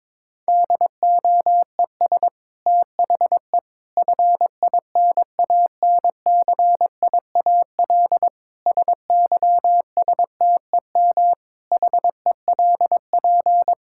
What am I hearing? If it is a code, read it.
Morse code, heard as a keyed tone, about 22 wpm, 700 Hz. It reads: DOES THE FINANCIAL SYSTEM HELP
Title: Sentences from Top 1000 Words - Review Part 5 - 22wpm